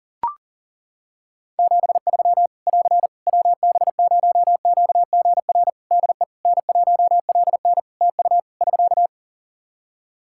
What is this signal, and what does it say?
Morse code, heard as a keyed tone, about 40 wpm, 700 Hz. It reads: 73 <AR> WB0QGR DE N1LN TU <SK>
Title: Practice QSO - Exchange 4 - Ending - 40wpm